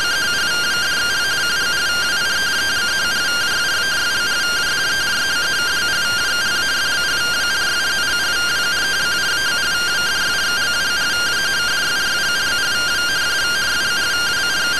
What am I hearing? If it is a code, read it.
Two-tone teleprinter signal, about 170 Hz shift.
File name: findings / MR AXIAL 71mm 3mm.gen-08 rtty rx